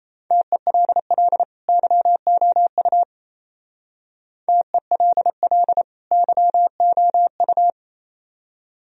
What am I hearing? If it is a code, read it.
Morse code, heard as a keyed tone, about 35 wpm, 700 Hz. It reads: TELL YOU TELL YOU
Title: Speed-Racing - Sets of 2 Words 28wpm